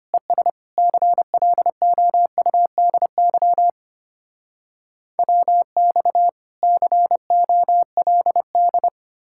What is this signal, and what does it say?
Morse code, heard as a keyed tone, about 33 wpm, 700 Hz. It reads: ES CLOUDY WX COLD